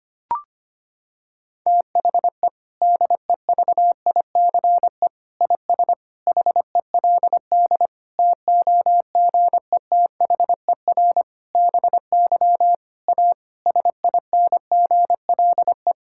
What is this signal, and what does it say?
Morse code, heard as a keyed tone, about 25 wpm, 700 Hz. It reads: THE DEVICE IS HELD TOGETHER BY A SINGLE
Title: Sentences from Top 1600 Words 25wpm